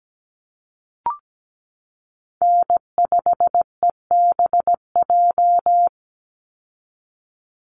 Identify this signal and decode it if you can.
Morse code, heard as a keyed tone, about 17 wpm, 700 Hz. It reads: N5EBJ
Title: USA Call Signs 17wpm